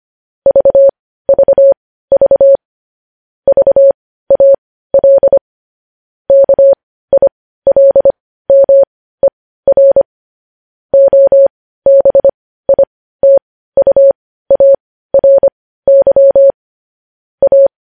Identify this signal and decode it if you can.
Morse code, heard as a keyed tone, about 25 wpm, spaced as for 15 wpm, 550 Hz. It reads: VVV VAL KILMER OBITUARY A
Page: news headlines in morse code at 15 wpm